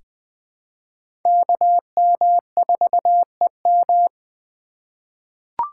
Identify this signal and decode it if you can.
Morse code, heard as a keyed tone, about 20 wpm, 700 Hz. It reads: KM4EM
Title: USA Call Signs - 5 Characters - 3x Character Spacing - 20wpm